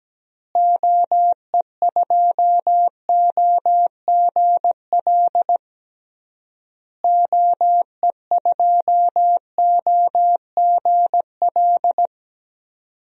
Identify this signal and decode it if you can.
Morse code, heard as a keyed tone, about 17 wpm, 700 Hz. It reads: OE2OGL OE2OGL
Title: DXCC-100 Call Signs - 6 Characters - Repeated 3x - 17wpm